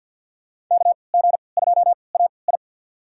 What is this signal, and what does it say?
Morse code, heard as a keyed tone, about 53 wpm, spaced as for 29 wpm, 700 Hz. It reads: KG2AI
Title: USA Call Signs - 5 Characters - 3x Character Spacing - 50wpm